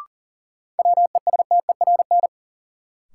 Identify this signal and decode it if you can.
Morse code, heard as a keyed tone, about 39 wpm, 700 Hz. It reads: WESTERN
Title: Sentences from Top 1000 Words - Review Part 7 - New Word Drill - 40wpm